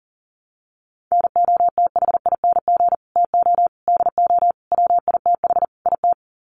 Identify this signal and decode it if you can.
Morse code, heard as a keyed tone, about 40 wpm, 700 Hz. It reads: NOTHING TO DO WITH IT
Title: Sets of 5 Words Encore 40wpm